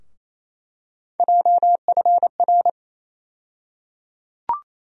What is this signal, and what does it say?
Morse code, heard as a keyed tone, about 28 wpm, 700 Hz. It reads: JFR
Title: USA Call Sign Postfixes - 3 Letters 28wpm